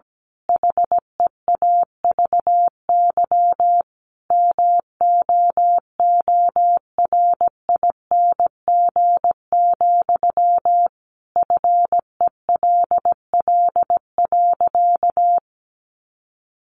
Morse code, heard as a keyed tone, about 17 wpm, 700 Hz. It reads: HEAVY MOORING, FELL.
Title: A Sound Of Thunder 17wpm